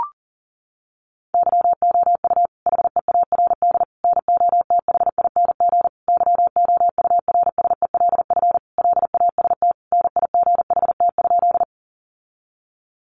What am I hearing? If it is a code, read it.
Morse code, heard as a keyed tone, about 37 wpm, 700 Hz. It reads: YOU HEARD NOTHING YOURSELF LAST NIGHT?
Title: Sentences from Top 600 Words 40wpm